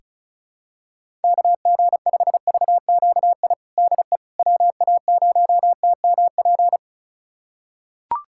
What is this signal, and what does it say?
Morse code, heard as a keyed tone, about 36 wpm, 700 Hz. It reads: KG5VQI DE WA0TMP
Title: USA Callsign DE Callsign 35wpm